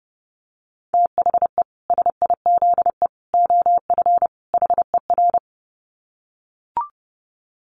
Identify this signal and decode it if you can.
Morse code, heard as a keyed tone, about 30 wpm, 700 Hz. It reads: THE SIZE OF HER
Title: Dit Challenge 30wpm